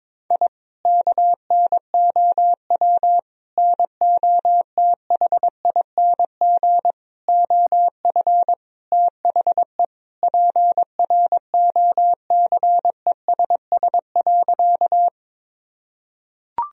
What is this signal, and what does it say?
Morse code, heard as a keyed tone, about 22 wpm, 700 Hz. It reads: I KNOW NOTHING OF THE PROCESS.
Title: Sentences from Top 1000 Words - Review Part 3 - 22wpm